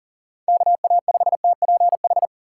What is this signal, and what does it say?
Morse code, heard as a keyed tone, about 40 wpm, 700 Hz. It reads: KA5TPH
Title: USA Call Signs - 2x Character Spacing - 40wpm